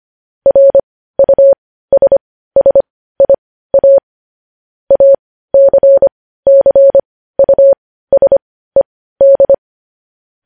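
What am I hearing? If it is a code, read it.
Morse code, heard as a keyed tone, about 25 wpm, spaced as for 15 wpm, 550 Hz. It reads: RUSSIA ACCUSED
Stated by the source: Morse code news headlines at 15 WPM.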